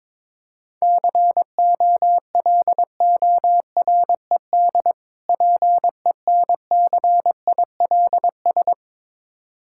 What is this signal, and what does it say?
Morse code, heard as a keyed tone, about 22 wpm, 690 Hz. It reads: COLORED PENCILS
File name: Difficult Words with Hints - 22wpm